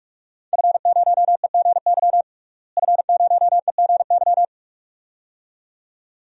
Morse code, heard as a keyed tone, about 46 wpm, 700 Hz. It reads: F0EGY F0EGY
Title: DXCC-100 Call Signs - 5 Characters - Repeated 2x - 45wpm